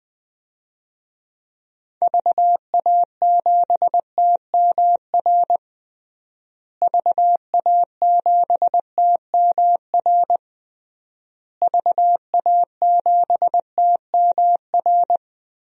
Morse code, heard as a keyed tone, about 20 wpm, 700 Hz. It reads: VA7TMR VA7TMR VA7TMR
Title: DXCC-100 Call Signs - 6 Characters - Repeated 3x - 20wpm